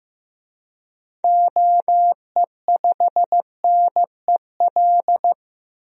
Morse code, heard as a keyed tone, about 15 wpm, 700 Hz. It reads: OE5NEL
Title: DXCC-100 Call Signs - 6 Characters - 15wpm